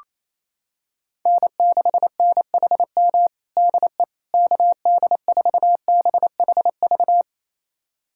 Morse code, heard as a keyed tone, about 28 wpm, 700 Hz. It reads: N6NHM DE KD4BHV
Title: USA Callsign DE Callsign 28wpm